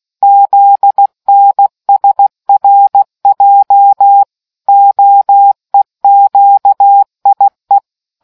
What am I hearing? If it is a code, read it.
Morse code, heard as a keyed tone, about 16 wpm, 800 Hz. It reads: ZNSRJ OEQIE